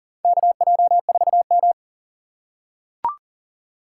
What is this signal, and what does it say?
Morse code, heard as a keyed tone, about 40 wpm, 700 Hz. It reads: KJ4M
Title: USA Call Signs - 4 Characters - 3x Character Spacing - 40wpm